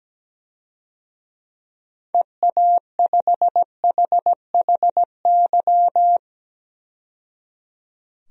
Morse code, heard as a keyed tone, about 17 wpm, 700 Hz. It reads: EA5HHY
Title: DXCC-100 Call Signs - 6 Characters - 17wpm